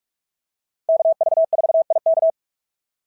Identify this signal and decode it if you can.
Morse code, heard as a keyed tone, about 46 wpm, 690 Hz. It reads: KV4IK
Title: SOTA QSO Practice 45wpm